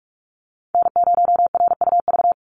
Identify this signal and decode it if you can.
Morse code, heard as a keyed tone, about 46 wpm, 700 Hz. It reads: N0RUV
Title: USA Call Signs 45wpm